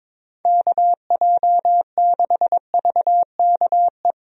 Morse code, heard as a keyed tone, about 22 wpm, 700 Hz. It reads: KJ6VKE
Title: USA Call Signs - 6 Characters - Repeated 3x - 22wpm